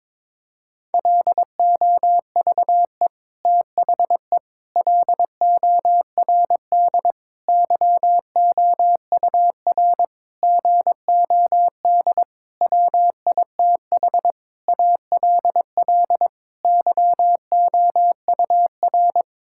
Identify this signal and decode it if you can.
Morse code, heard as a keyed tone, about 22 wpm, 700 Hz. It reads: LOVE THE LORD YOUR GOD WITH ALL YOUR
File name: Sets of 9 Words - 22wpm